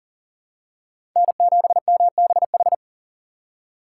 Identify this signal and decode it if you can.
Morse code, heard as a keyed tone, about 40 wpm, 700 Hz. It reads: N7MBH
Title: USA Call Signs - 5 Characters - 2x Character Spacing - 40wpm